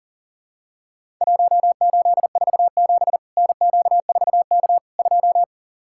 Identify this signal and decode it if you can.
Morse code, heard as a keyed tone, about 40 wpm, 700 Hz. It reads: 1847 NQ4K 2
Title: Worked All Europe DX Contest - QTC Exchange - 40wpm